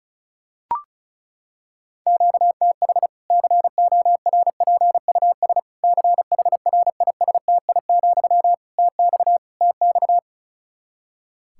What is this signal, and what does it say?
Morse code, heard as a keyed tone, about 34 wpm, 700 Hz. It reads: QTH CORPUS CHRISTI, TX TX
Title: K1USN SST QSO Practice 35wpm